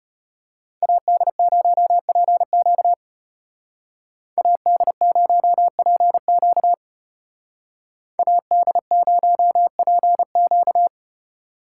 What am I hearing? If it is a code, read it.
Morse code, heard as a keyed tone, about 31 wpm, 700 Hz. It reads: AD0PQ AD0PQ AD0PQ
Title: Speed Racing - USA Call Signs 25wpm